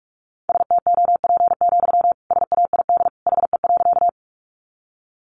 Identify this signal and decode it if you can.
Morse code, heard as a keyed tone, about 44 wpm, 700 Hz. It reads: STOP, SAID HE.
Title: Sentences from Top 300 Words 45wpm